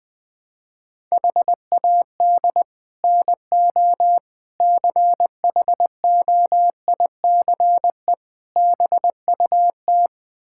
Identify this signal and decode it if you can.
Morse code, heard as a keyed tone, about 21 wpm, 700 Hz. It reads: HAD NO CHOICE BUT
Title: Build A Phrase - 5-Word Phrases - 20wpm